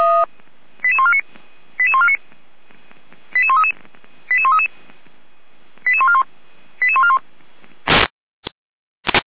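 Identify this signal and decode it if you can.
Five-tone selective call (ZVEI), heard as quick runs of five tones, tones of about 70 ms. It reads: 88149 88149 88133 88133 88141 88141
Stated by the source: ZVEI five-tone selective call 88501, received on SDR.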